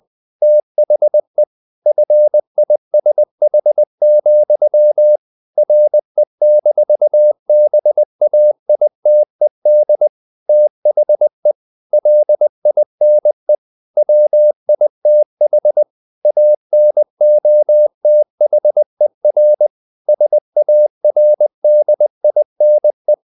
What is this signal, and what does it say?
Morse code, heard as a keyed tone, about 20 wpm, 600 Hz. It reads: THE FISH, RE-BAITED THE LINE WITH ANOTHER SARDINE
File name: the old man and the sea part 2 20wpm